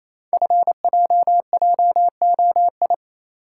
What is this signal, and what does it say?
Morse code, heard as a keyed tone, about 29 wpm, 690 Hz. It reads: FJJOI
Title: Code Groups - 5 Letters - 28wpm